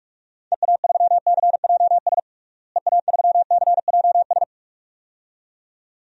Morse code, heard as a keyed tone, about 46 wpm, 690 Hz. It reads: EA3CJS EA3CJS
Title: DXCC-100 Call Signs - 6 Characters - Repeated 2x - 45wpm